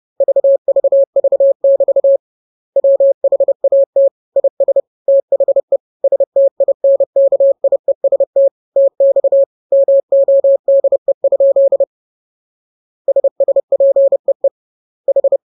Morse code, heard as a keyed tone, about 30 wpm, 550 Hz. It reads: VVV <BT> WHAT IS THE STINKIEST TX MODE? SSPEE H
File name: jokes / joke 27 30 0000